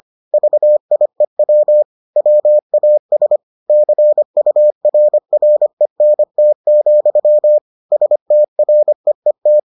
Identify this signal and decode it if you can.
Morse code, heard as a keyed tone, about 25 wpm, 600 Hz. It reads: VIEW WAS CURRENT, STREET